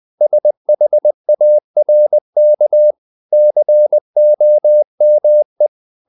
Morse code, heard as a keyed tone, about 20 wpm, 600 Hz. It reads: SHARK COME
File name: the old man and the sea part 2 20wpm